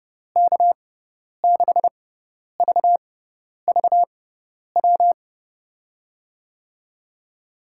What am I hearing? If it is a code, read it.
Morse code, heard as a keyed tone, about 30 wpm, spaced as for 10 wpm, 700 Hz. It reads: K6VVW
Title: USA Call Signs - 6 Characters - 6x Character Spacing - 30wpm